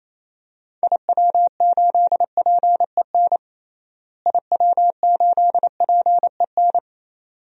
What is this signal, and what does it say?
Morse code, heard as a keyed tone, about 28 wpm, 700 Hz. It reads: IW8PEN IW8PEN
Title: DXCC-100 Call Signs - 6 Characters - Repeated 2x - 28wpm